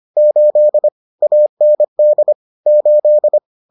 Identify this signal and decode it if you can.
Morse code, heard as a keyed tone, about 25 wpm, 600 Hz. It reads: 8 AND 8